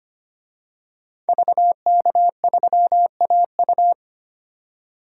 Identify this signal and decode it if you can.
Morse code, heard as a keyed tone, about 25 wpm, 690 Hz. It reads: VK3AU
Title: DXCC-100 Call Signs - 5 Characters - 25wpm